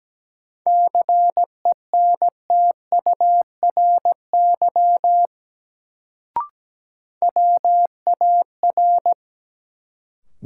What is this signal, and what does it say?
Morse code, heard as a keyed tone, about 17 wpm, 700 Hz. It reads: CENTURY WAR
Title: Sets of 5 Words - New Word Drill - 17wpm